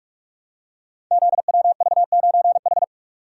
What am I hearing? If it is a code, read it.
Morse code, heard as a keyed tone, about 45 wpm, 700 Hz. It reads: ZWV9H
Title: Code Groups - 5 Letter:Numbers - 45wpm